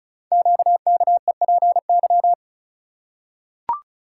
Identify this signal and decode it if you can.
Morse code, heard as a keyed tone, about 36 wpm, 700 Hz. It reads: QKEPY
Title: Code Groups - 5 Letters - 35wpm